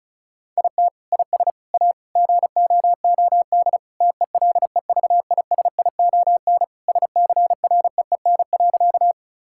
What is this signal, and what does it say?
Morse code, heard as a keyed tone, about 36 wpm, 700 Hz. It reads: IT IS A GOOD TELEVISION SCREEN.
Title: Sentences from Top 1000 Words - Review Part 6 - 35wpm